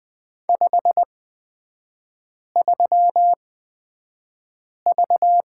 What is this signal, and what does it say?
Morse code, heard as a keyed tone, about 20 wpm, spaced as for 5 wpm, 700 Hz. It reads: 53V
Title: 236 Lesson 38e - Letter q - Three Characters 20wpm